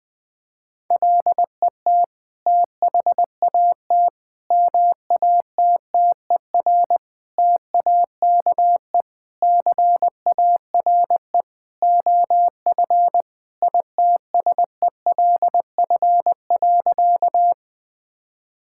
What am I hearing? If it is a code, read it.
Morse code, heard as a keyed tone, about 20 wpm, 700 Hz. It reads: LET THAT MATTER TAKE CARE OF ITSELF.
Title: Sentences from Top 1000 Words - Review Part 5 - 20wpm